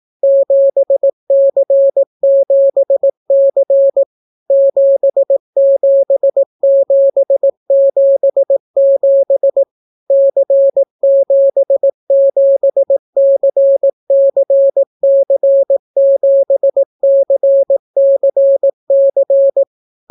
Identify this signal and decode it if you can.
Morse code, heard as a keyed tone, about 18 wpm, 550 Hz. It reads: 7C7C 77777 C77CCC7CCC